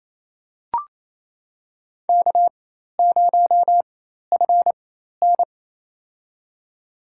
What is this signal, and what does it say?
Morse code, heard as a keyed tone, about 27 wpm, spaced as for 13 wpm, 700 Hz. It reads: K0FN
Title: USA Call Signs - 4x Character Spacing - 28wpm